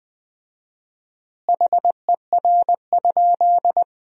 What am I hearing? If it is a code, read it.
Morse code, heard as a keyed tone, about 20 wpm, 700 Hz. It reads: HER?
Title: 4x Word-Spacing - Sentences from Top 100 Words - 20wpm